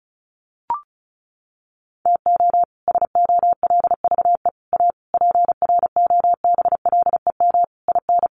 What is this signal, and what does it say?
Morse code, heard as a keyed tone, about 34 wpm, 700 Hz. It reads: TO SOLVE A PROBLEM IN
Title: Sentences from Top 1400 Words 35wpm